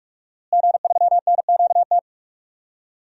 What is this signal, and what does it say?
Morse code, heard as a keyed tone, about 46 wpm, 700 Hz. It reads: G3NQT
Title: DXCC-100 Call Signs - 5 Characters - Repeated 2x - 45wpm